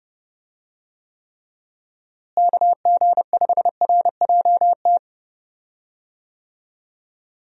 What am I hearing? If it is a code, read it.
Morse code, heard as a keyed tone, about 30 wpm, 700 Hz. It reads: KG5RJT
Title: USA Call Signs - 6 Characters - Repeated 2x - 30wpm